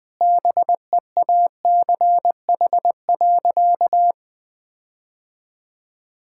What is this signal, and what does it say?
Morse code, heard as a keyed tone, about 20 wpm, 700 Hz. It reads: BEACH.
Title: Sentences with 6-Word Phrases - 20wpm